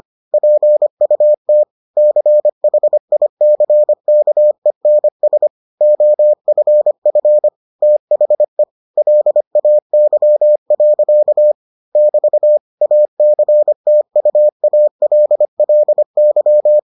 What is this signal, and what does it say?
Morse code, heard as a keyed tone, about 25 wpm, 600 Hz. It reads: PUT CHICKENS OFF THE LAY. <BT> ACTUALLY